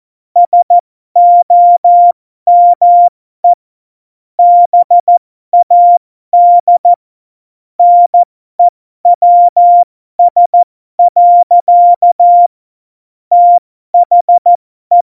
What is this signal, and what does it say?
Morse code, heard as a keyed tone, about 14 wpm, spaced as for 12 wpm, 700 Hz. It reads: SOME BAD NEWS. THE
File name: Morsekurs / Witze